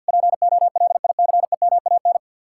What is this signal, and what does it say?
Morse code, heard as a keyed tone, about 64 wpm, spaced as for 56 wpm, 700 Hz. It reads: POLICEMAN
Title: Speed-Racing Common QSO Phrases - 50wpm